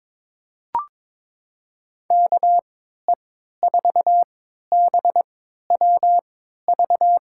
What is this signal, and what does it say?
Morse code, heard as a keyed tone, about 22 wpm, spaced as for 13 wpm, 700 Hz. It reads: KE4BWV
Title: USA Call Signs - 6 Characters - 3x Character Spacing - 22wpm